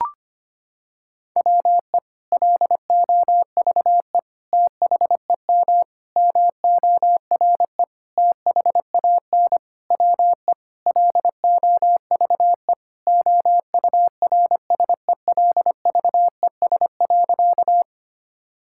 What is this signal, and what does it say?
Morse code, heard as a keyed tone, about 25 wpm, 700 Hz. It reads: WE LOVE THEM MORE THAN WE LOVE OURSELVES.
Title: Sentences from Top 1600 Words 25wpm